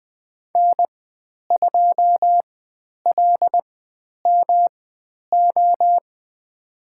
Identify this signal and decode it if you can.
Morse code, heard as a keyed tone, about 20 wpm, spaced as for 10 wpm, 700 Hz. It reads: N2LMO
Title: 109 Lesson 18g - Letter m - Call Signs 20wpm